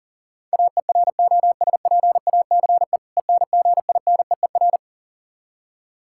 Morse code, heard as a keyed tone, about 40 wpm, 700 Hz. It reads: AEROSPACE ENGINEER
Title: Practice QSO - Exchange 3 - More Info - 40wpm